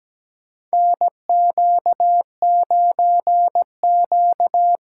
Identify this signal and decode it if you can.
Morse code, heard as a keyed tone, about 17 wpm, 700 Hz. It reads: NQ9Q
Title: USA Call Signs - 4 Characters - 4x Character Spacing - 17wpm